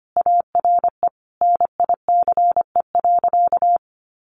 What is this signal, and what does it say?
Morse code, heard as a keyed tone, about 25 wpm, 700 Hz. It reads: ARE NICE.
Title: Easy Phrases 25wpm